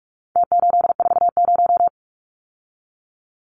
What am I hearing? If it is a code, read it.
Morse code, heard as a keyed tone, about 44 wpm, 700 Hz. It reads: T840
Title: Numbers - Contest Style - 4-Digit 45wpm